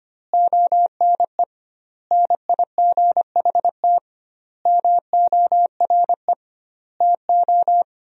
Morse code, heard as a keyed tone, about 25 wpm, 700 Hz. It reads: ONE NIGHT MORE TO
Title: 2x Word-Spacing - Sentences from Top 300 Words - 25wpm